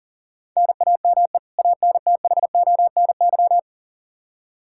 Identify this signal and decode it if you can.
Morse code, heard as a keyed tone, about 39 wpm, 700 Hz. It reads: NAME ANTHONY
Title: K1USN SST QSO Practice 40wpm